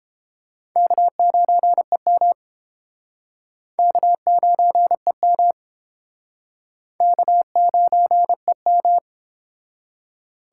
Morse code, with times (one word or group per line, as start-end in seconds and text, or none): 0.8-2.3 K9EM
3.8-5.5 K9EM
7.0-9.0 K9EM